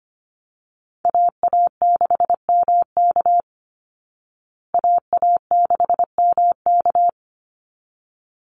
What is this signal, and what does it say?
Morse code, heard as a keyed tone, about 25 wpm, 700 Hz. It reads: AA6MK AA6MK
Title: Call Signs Repeated 3x - 25wpm